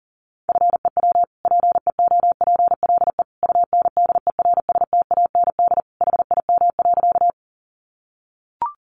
Morse code, heard as a keyed tone, about 40 wpm, 700 Hz. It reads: FEW PEOPLE UNDERSTAND HIM.
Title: Sentences from Top 600 Words 40wpm